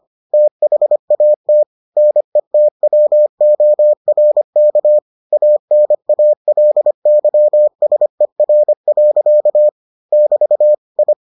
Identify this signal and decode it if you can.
Morse code, heard as a keyed tone, about 25 wpm, 600 Hz. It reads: THAT NETWORK ANALYSER. <BT> I